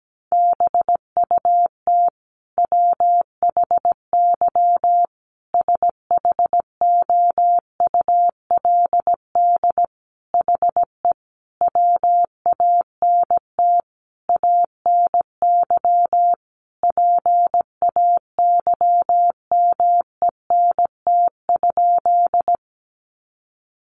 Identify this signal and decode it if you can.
Morse code, heard as a keyed tone, about 17 wpm, 700 Hz. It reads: BUT WHY SHOULD HE WANT ANY PAYMENT?
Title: Sentences from Top 1200 Words 17wpm